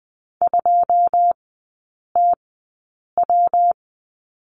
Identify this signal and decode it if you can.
Morse code, heard as a keyed tone, about 20 wpm, spaced as for 8 wpm, 700 Hz. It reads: 2TW
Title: 138 Lesson 23e - Letter f - Three Characters 20wpm